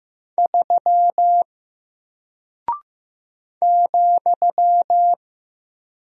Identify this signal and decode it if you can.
Morse code, heard as a keyed tone, about 15 wpm, 700 Hz. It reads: 3 ,
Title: Single Character (Letter:Number:Punctuation:Prosign) Alt 15wpm